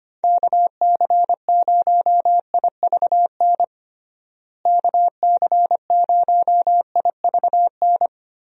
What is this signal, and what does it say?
Morse code, heard as a keyed tone, about 25 wpm, 700 Hz. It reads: KC0IVN KC0IVN
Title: USA Call Signs - 6 Characters - Repeated 2x - 25wpm